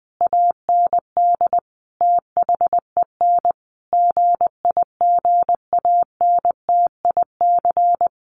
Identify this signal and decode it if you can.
Morse code, heard as a keyed tone, about 20 wpm, 700 Hz. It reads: AND THEN GIGANTIC
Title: A Sound Of Thunder 20wpm